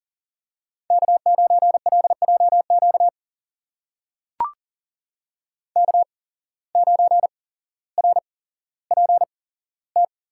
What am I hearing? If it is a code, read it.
Morse code, heard as a keyed tone, about 40 wpm, spaced as for 17 wpm, 700 Hz. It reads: K9LJQ K9RPT
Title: USA Call Signs - 8x Character Spacing - 40wpm